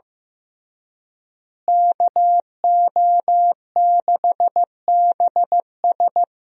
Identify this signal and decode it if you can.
Morse code, heard as a keyed tone, about 15 wpm, 700 Hz. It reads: KO6BS
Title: USA Call Signs - 5 Characters - Repeated 3x - 15wpm